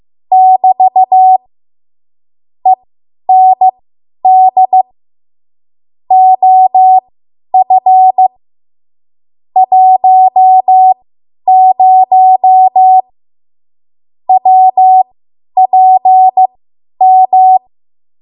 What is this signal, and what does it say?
Morse code, heard as a keyed tone, about 15 wpm, spaced as for 10 wpm, 750 Hz. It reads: <BT> END OF 10 WPM
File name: Morse / 10WPM